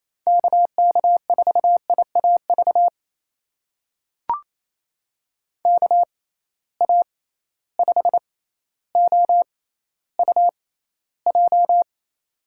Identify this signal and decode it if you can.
Morse code, heard as a keyed tone, about 28 wpm, spaced as for 14 wpm, 700 Hz. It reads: KK4IAV KA5OUJ
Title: USA Call Signs - 6 Characters - 6x Character Spacing - 28wpm